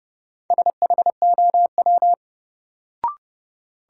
Morse code, heard as a keyed tone, about 30 wpm, 700 Hz. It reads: SHOW
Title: Sentences from Top 200 Words - New Word Drill - 30wpm